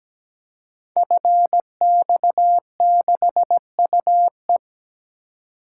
Morse code, heard as a keyed tone, about 17 wpm, 700 Hz. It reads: FX6UE
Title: Code Groups - 5 Letter:Numbers - 17wpm